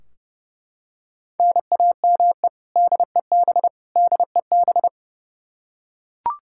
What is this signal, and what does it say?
Morse code, heard as a keyed tone, about 30 wpm, 700 Hz. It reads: NAME DEB DEB
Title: Practice QSO - Exchange 1 - Beginning - 30wpm